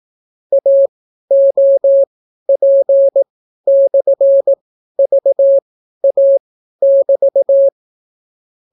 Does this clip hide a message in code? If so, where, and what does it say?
Yes: Morse code from 0.5 to 7.7 s, heard as a keyed tone, about 18 wpm, spaced as for 12 wpm, 550 Hz. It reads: AOP/VA<BT>